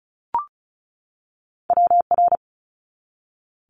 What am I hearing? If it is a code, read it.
Morse code, heard as a keyed tone, about 36 wpm, 690 Hz. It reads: WR
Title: Multi-letter Phonemes 35wpm